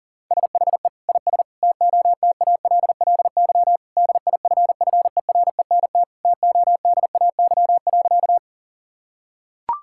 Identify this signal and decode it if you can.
Morse code, heard as a keyed tone, about 40 wpm, 700 Hz. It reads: SHE IS TOTALLY DIFFERENT TODAY.
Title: Sentences from Top 1600 Words 40wpm